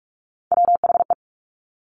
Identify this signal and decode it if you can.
Morse code, heard as a keyed tone, about 46 wpm, 700 Hz. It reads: WHE
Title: Top Three Letter Combinations in Words 45wpm